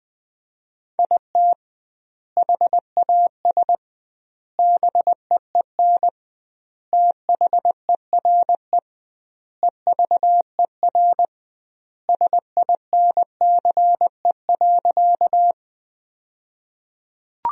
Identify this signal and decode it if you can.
Morse code, heard as a keyed tone, about 20 wpm, 700 Hz. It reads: IT HAS BEEN THERE EVER SINCE.
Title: 2x Word-Spacing - Sentences from Top 400 Words - 20wpm